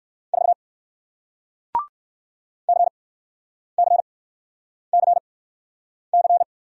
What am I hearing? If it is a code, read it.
Morse code, heard as a keyed tone, about 60 wpm, spaced as for 9 wpm, 700 Hz. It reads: 4 CCCC